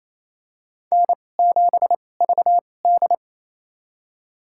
Morse code, heard as a keyed tone, about 28 wpm, spaced as for 20 wpm, 700 Hz. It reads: N7VD
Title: USA Call Signs - 4 Characters - 2x Character Spacing - 28wpm